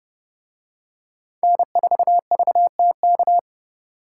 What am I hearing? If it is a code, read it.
Morse code, heard as a keyed tone, about 31 wpm, 700 Hz. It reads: N4VTK